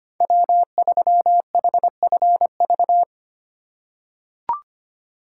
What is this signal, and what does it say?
Morse code, heard as a keyed tone, about 25 wpm, 700 Hz. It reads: W3HFV
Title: USA Call Signs - 5 Characters - 2x Character Spacing - 25wpm